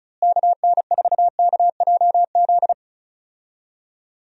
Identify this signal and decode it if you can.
Morse code, heard as a keyed tone, about 34 wpm, 700 Hz. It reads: KN4KJZ
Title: USA Call Signs - 6 Characters - Repeated 2x - 35wpm